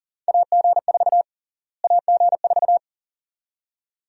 Morse code, heard as a keyed tone, about 38 wpm, 700 Hz. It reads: AG4 AG4
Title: USA Call Sign Prefixes - Repeated 2x - 40wpm